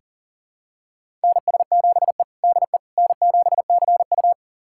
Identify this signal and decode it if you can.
Morse code, heard as a keyed tone, about 40 wpm, 700 Hz. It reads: NS7E DE N7CU
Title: USA Callsign DE Callsign - 4 Characters - 40wpm